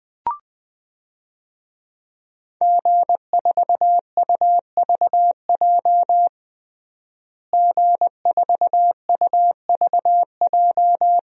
Morse code, heard as a keyed tone, about 21 wpm, 700 Hz. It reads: G4UVJ G4UVJ
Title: DXCC-100 Call Signs - 5 Characters - Repeated 3x - 20wpm